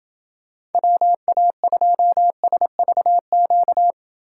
Morse code, heard as a keyed tone, about 28 wpm, 700 Hz. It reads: WA2SVQ